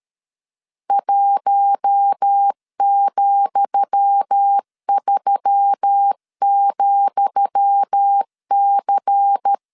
Morse code, heard as a keyed tone, about 13 wpm, 800 Hz. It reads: 1,3,C
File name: LEZIONI / 10 SIGNS / 10D COMA 12wpm 2544